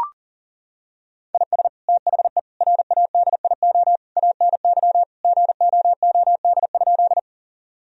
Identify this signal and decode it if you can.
Morse code, heard as a keyed tone, about 40 wpm, 700 Hz. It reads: IS THE RADIO ANY GOOD?
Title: Sentences from Top 1000 Words - Review Part 3 - 40wpm